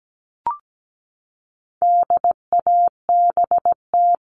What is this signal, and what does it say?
Morse code, heard as a keyed tone, about 17 wpm, 700 Hz. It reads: DABT
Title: All Cut Numbers - 4-Digit 17wpm